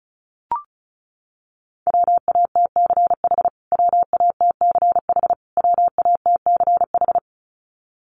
Morse code, heard as a keyed tone, about 34 wpm, 700 Hz. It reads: WATCH WATCH WATCH
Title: Top 500 Words Repeated 35wpm